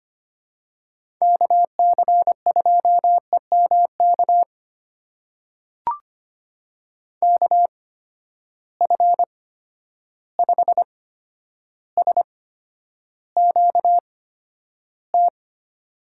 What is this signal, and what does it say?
Morse code, heard as a keyed tone, about 25 wpm, spaced as for 11 wpm, 700 Hz. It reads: KC2EMK KF5SQT
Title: USA Call Signs - 6 Characters - 8x Character Spacing - 25wpm